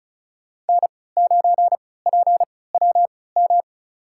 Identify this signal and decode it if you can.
Morse code, heard as a keyed tone, about 36 wpm, spaced as for 20 wpm, 700 Hz. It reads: N9PWM
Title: USA Call Signs - 3x Character Spacing - 35wpm